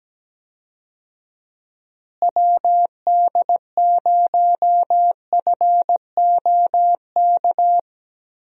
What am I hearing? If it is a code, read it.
Morse code, heard as a keyed tone, about 17 wpm, 700 Hz. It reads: WD0FOK
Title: USA Call Signs - 6 Characters - Repeated 3x - 17wpm